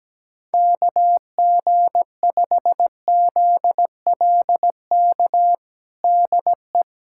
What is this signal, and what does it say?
Morse code, heard as a keyed tone, about 17 wpm, 700 Hz. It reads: KG5ZLK DE
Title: USA Callsign DE Callsign - 6 Characters - 17wpm